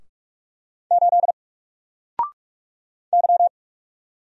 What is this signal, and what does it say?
Morse code, heard as a keyed tone, about 44 wpm, 700 Hz. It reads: 8 Y
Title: Single Letter-Number Alt 45wpm